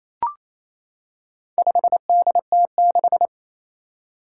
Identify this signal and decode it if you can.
Morse code, heard as a keyed tone, about 28 wpm, 700 Hz. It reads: 5DT6
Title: Four Challenging Letter-Number Combinations 28wpm